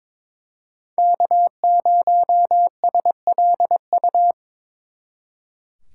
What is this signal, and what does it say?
Morse code, heard as a keyed tone, about 26 wpm, 700 Hz. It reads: K0SLU
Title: Speed Racing - USA Call Signs 22wpm